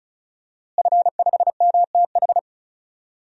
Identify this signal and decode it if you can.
Morse code, heard as a keyed tone, about 36 wpm, 700 Hz. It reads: F5MTH
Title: DXCC-100 Call Signs - 5 Characters - Repeated 2x - 35wpm